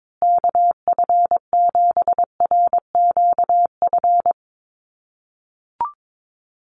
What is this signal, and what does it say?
Morse code, heard as a keyed tone, about 22 wpm, 700 Hz. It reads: KF7RQF
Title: USA Call Signs Repeated 2x 22wpm